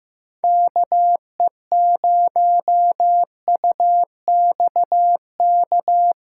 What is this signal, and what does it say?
Morse code, heard as a keyed tone, about 15 wpm, 700 Hz. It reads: KE0UXK
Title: USA Call Signs - 8x Character Spacing - 15wpm